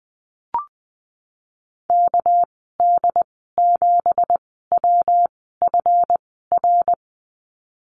Morse code, heard as a keyed tone, about 19 wpm, spaced as for 14 wpm, 700 Hz. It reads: KD7WFR
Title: 207 Lesson 33g - Letter j - Call Signs 20wpm